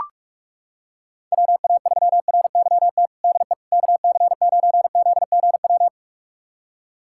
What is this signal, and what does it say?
Morse code, heard as a keyed tone, about 42 wpm, 700 Hz. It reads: WA3RYT DE KC9ZGW
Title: USA Callsign DE Callsign 45wpm